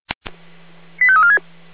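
Five-tone selective call (ZVEI), heard as quick runs of five tones, tones of about 70 ms. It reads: 85346
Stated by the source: ZVEI five-tone selective call received on an SDR.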